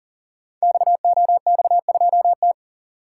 Morse code, heard as a keyed tone, about 42 wpm, 690 Hz. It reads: XOX2T
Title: Code Groups - 5 Letter:Numbers - 40wpm